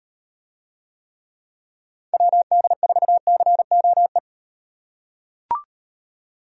Morse code, heard as a keyed tone, about 38 wpm, 700 Hz. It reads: WD4COE